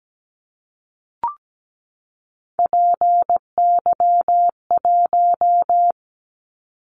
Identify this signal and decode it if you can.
Morse code, heard as a keyed tone, about 17 wpm, 700 Hz. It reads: PY1
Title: Callsign Prefixes for Top 100 DXCC - 17wpm